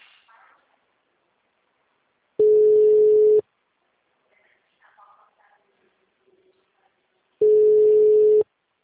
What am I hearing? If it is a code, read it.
Ringback tone (Europe, 425 Hz).
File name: out- -2041- - - .64408